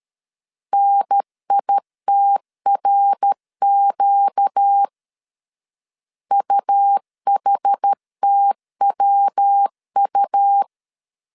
Morse code, heard as a keyed tone, about 13 wpm, 800 Hz. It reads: NITRQ UHTWU
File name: CW / DET / 1 6 1500C S A 12wpm 3440